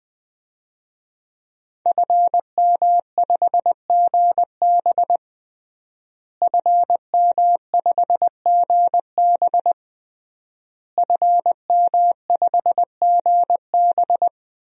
Morse code, heard as a keyed tone, about 20 wpm, 700 Hz. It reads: FM5GB FM5GB FM5GB
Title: DXCC-100 Call Signs - 5 Characters - Repeated 3x - 20wpm